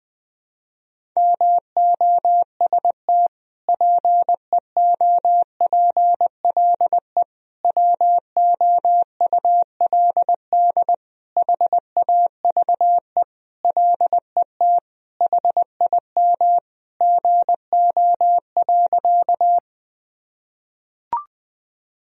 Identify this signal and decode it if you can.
Morse code, heard as a keyed tone, about 20 wpm, 700 Hz. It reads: MOST PEOPLE WOULD HAVE LET HIM GO.
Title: Sentences from Top 1000 Words - Review Part 1 - 20wpm